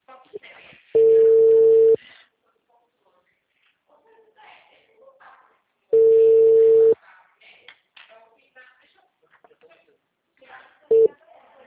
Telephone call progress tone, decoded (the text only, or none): ringback tone